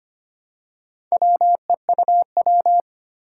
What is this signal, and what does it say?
Morse code, heard as a keyed tone, about 25 wpm, 700 Hz. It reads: WEUW
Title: Four Challenging Letter-Number Combinations 25wpm